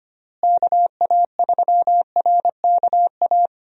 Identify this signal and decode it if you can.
Morse code, heard as a keyed tone, about 25 wpm, 700 Hz. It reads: KA3RKA
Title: USA Call Signs - 6 Characters - Repeated 2x - 25wpm